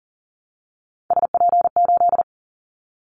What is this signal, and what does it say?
Morse code, heard as a keyed tone, about 38 wpm, 700 Hz. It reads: SP8
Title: Callsign Prefixes for Top 25 DXCC - 40wpm